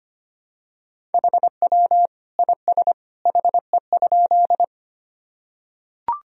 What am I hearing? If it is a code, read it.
Morse code, heard as a keyed tone, about 25 wpm, 700 Hz. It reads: HW IS HE?
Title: Sentences from Top 100 Words and Abbreviations 25wpm